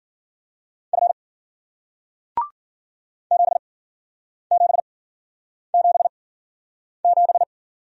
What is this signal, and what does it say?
Morse code, heard as a keyed tone, about 53 wpm, spaced as for 9 wpm, 700 Hz. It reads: F 7777